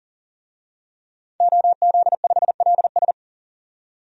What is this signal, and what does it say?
Morse code, heard as a keyed tone, about 40 wpm, 700 Hz. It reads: OZ5LS
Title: DXCC-100 Call Signs - 5 Characters - 40wpm